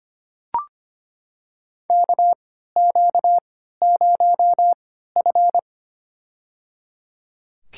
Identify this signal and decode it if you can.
Morse code, heard as a keyed tone, about 25 wpm, spaced as for 14 wpm, 700 Hz. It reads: KQ0F
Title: USA Call Signs - 4 Characters - 3x Character Spacing - 25wpm